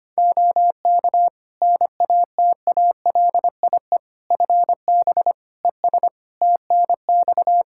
Morse code, heard as a keyed tone, about 25 wpm, 700 Hz. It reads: OK NATALIE FB ES TNX
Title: Practice QSO - Exchange 2 - Info - 25wpm